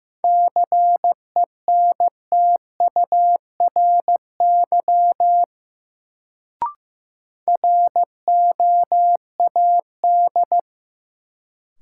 Morse code, heard as a keyed tone, about 15 wpm, 700 Hz. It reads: CENTURY ROAD
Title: Sets of 5 Words - New Word Drill - 15wpm